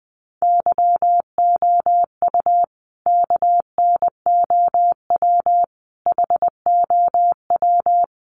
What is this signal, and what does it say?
Morse code, heard as a keyed tone, about 20 wpm, 700 Hz. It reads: YOU KNOW HOW
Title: Sentences from Top 1100 Words 20wpm